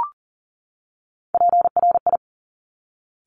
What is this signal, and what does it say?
Morse code, heard as a keyed tone, about 38 wpm, 690 Hz. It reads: PRI
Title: Top Three Letter Combinations in Words 40wpm